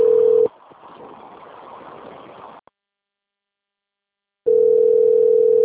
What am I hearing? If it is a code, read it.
Ringback tone (North America), 440 + 480 Hz.